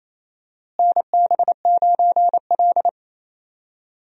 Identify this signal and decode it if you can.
Morse code, heard as a keyed tone, about 28 wpm, 700 Hz. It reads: NB9L
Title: USA Call Signs - 4 Characters - Repeated 2x - 28wpm